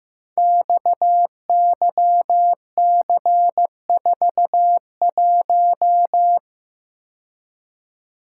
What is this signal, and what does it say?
Morse code, heard as a keyed tone, about 15 wpm, 700 Hz. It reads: XYC41
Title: Code Groups - 5 Letter:Numbers - 15wpm